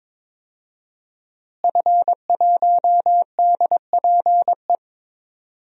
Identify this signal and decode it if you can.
Morse code, heard as a keyed tone, about 22 wpm, 700 Hz. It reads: F1DPE
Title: DXCC-100 Call Signs - 5 Characters - Repeated 3x - 22wpm